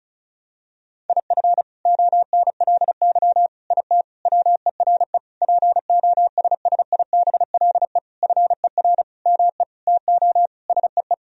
Morse code, heard as a keyed tone, about 36 wpm, 700 Hz. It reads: IF ONLY IT WERE POSSIBLE FER ME TO SEE
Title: Sentences from Top 500 Words and Abbreviations 35wpm